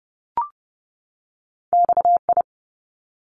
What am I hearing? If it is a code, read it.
Morse code, heard as a keyed tone, about 30 wpm, 700 Hz. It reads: XI
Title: Top Two Letter Combinations in Words 30wpm